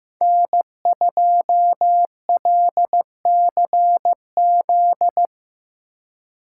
Morse code, heard as a keyed tone, about 17 wpm, 700 Hz. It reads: N2LCZ
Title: Speed Racing - USA Call Signs 15wpm